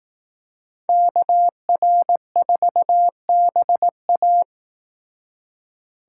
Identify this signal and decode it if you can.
Morse code, heard as a keyed tone, about 18 wpm, 700 Hz. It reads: KR4BA
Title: Speed Racing - USA 2x2 Call Signs 15wpm